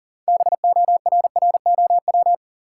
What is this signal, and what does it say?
Morse code, heard as a keyed tone, about 37 wpm, 700 Hz. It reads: BORROW
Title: Sentences from Top 1500 Words - New Word Drill Alt - 40wpm